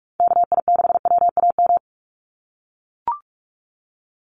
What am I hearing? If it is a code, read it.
Morse code, heard as a keyed tone, about 44 wpm, 700 Hz. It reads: KI6WAM
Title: USA Call Signs 45wpm